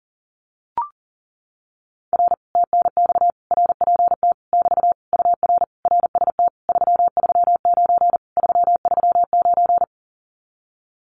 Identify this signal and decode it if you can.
Morse code, heard as a keyed tone, about 40 wpm, 700 Hz. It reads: R TNX RPT <BT> UR RST 339 339
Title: Common QSO Phrases Encore 40wpm